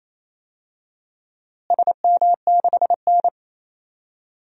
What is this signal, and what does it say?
Morse code, heard as a keyed tone, about 28 wpm, 700 Hz. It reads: SM6N
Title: DXCC-100 Call Signs - 4 Characters - Repeated 3x - 28wpm